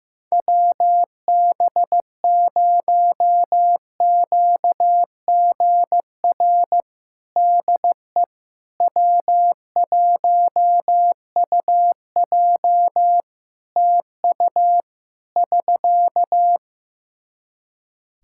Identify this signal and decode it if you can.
Morse code, heard as a keyed tone, about 16 wpm, 700 Hz. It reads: WB0QGR DE W1UJ TU <SK>
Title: Practice QSO - Exchange 4 - Ending - 15wpm